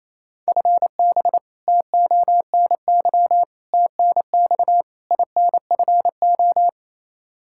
Morse code, heard as a keyed tone, about 27 wpm, 700 Hz. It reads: FB TONY TNX INFO
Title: Practice QSO - Exchange 3 - More Info - 28wpm